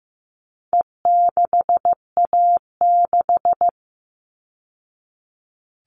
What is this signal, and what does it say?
Morse code, heard as a keyed tone, about 15 wpm, 700 Hz. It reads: E6A6
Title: All Cut Numbers - 4-Digit 15wpm